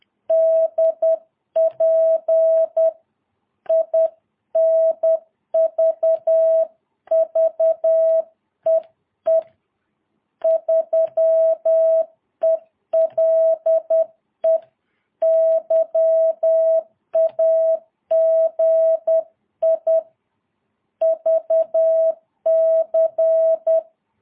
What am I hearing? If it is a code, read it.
Morse code, heard as a keyed tone, about 10 wpm, 660 Hz. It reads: DP INVVEE 3ELE YAGI VC
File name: jh8cbh / 2022cw